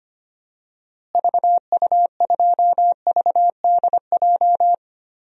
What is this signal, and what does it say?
Morse code, heard as a keyed tone, about 25 wpm, 700 Hz. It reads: VU2VDJ
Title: DXCC-100 Call Signs - 6 Characters - Repeated 2x - 25wpm